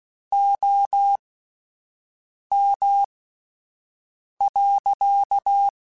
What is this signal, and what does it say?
Morse code, heard as a keyed tone, about 16 wpm, spaced as for 5 wpm, 800 Hz. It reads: OM.
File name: (spaced) / QSOs Spaced-mp3s / kc7haa @10 sp